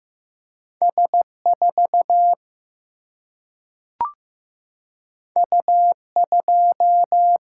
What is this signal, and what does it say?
Morse code, heard as a keyed tone, about 15 wpm, 700 Hz. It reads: S 4 U 2
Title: Single-Letter Single-Number 15wpm